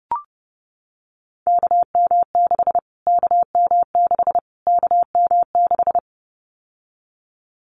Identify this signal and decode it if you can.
Morse code, heard as a keyed tone, about 30 wpm, 700 Hz. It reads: KM6 KM6 KM6
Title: Callsign Prefixes for Top 25 DXCC - Repeated 3x - 30wpm